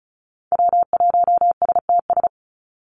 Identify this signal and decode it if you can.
Morse code, heard as a keyed tone, about 34 wpm, 700 Hz. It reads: W1STS
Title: USA Call Signs 35wpm